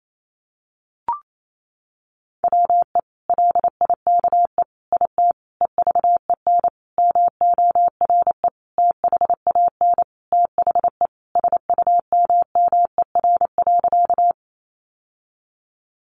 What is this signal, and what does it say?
Morse code, heard as a keyed tone, about 28 wpm, 700 Hz. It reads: WE LIKE IT EVEN MORE THAN THE SUMMER.
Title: Sentences from Top 1100 Words 28wpm